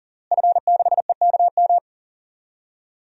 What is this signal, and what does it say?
Morse code, heard as a keyed tone, about 40 wpm, 700 Hz. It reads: F6EKM
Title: DXCC-100 Call Signs - 5 Characters - Repeated 2x - 40wpm